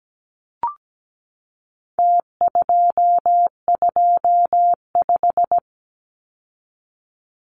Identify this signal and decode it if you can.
Morse code, heard as a keyed tone, about 17 wpm, 700 Hz. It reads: T225
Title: Numbers - Contest Style - 4-Digit 17wpm